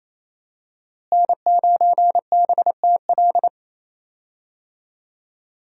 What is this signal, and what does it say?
Morse code, heard as a keyed tone, about 27 wpm, 700 Hz. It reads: N9BTL
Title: USA Call Signs - 5 Characters - Repeated 3x - 28wpm